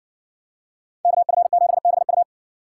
Morse code, heard as a keyed tone, about 57 wpm, 700 Hz. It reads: KF7BU